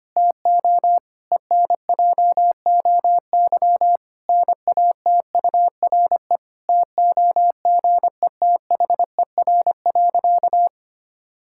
Morse code, heard as a keyed tone, about 25 wpm, 700 Hz. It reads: TO ENJOY NATURE TOGETHER.
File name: practice / 25wpm / Difficult Sentences with Hints - 25wpm